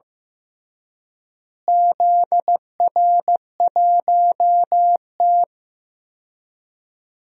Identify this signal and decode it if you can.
Morse code, heard as a keyed tone, about 15 wpm, 700 Hz. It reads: ZR1T
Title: DXCC-100 Call Signs - 4 Characters - Repeated 3x - 15wpm